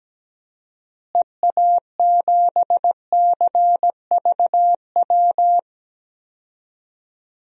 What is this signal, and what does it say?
Morse code, heard as a keyed tone, about 17 wpm, 700 Hz. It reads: EA7CVW
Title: DXCC-100 Call Signs - 6 Characters - Repeated 3x - 17wpm